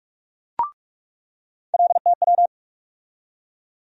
Morse code, heard as a keyed tone, about 46 wpm, 700 Hz. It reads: LTW
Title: USA Call Sign Postfixes - 3 Letters 45wpm